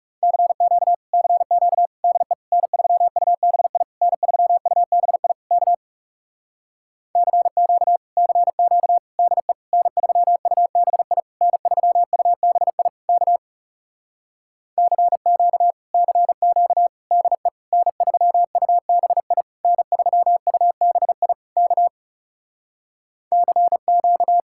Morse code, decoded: CQ CQ DE N3UBI N3UBI K CQ CQ DE N3UBI N3UBI K CQ CQ DE N3UBI N3UBI K CQ

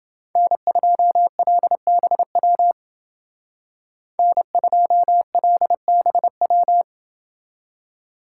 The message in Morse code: N2LBW N2LBW

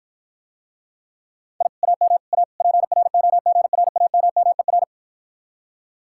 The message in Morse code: I AM A PROGRAMMER

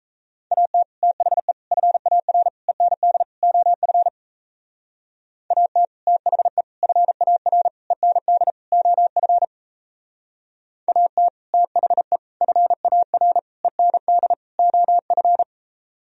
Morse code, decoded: AT THE FAR END OF AT THE FAR END OF AT THE FAR END OF

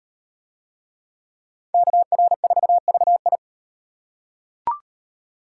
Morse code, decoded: KR4VI

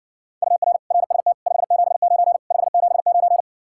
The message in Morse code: UR RST 579 579

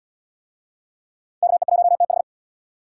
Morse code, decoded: KE1TH

